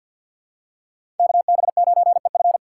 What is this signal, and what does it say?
Morse code, heard as a keyed tone, about 52 wpm, 690 Hz. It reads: KB9EF
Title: USA Call Signs - 2x Character Spacing - 50wpm